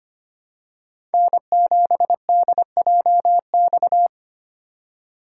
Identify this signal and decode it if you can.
Morse code, heard as a keyed tone, about 25 wpm, 700 Hz. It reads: N7DJX
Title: USA Call Signs - 5 Characters - 3x Character Spacing - 25wpm